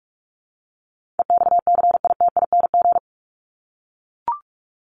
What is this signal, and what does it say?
Morse code, heard as a keyed tone, about 44 wpm, 700 Hz. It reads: EXCITING
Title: Top 1751-2000 Words 45wpm